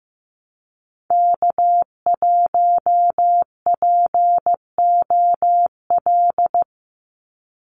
Morse code, heard as a keyed tone, about 15 wpm, 700 Hz. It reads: K1POL
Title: USA Call Signs 15wpm